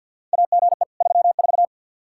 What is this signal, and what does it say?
Morse code, heard as a keyed tone, about 50 wpm, 700 Hz. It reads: AGE 34
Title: K1USN SST QSO Practice 50wpm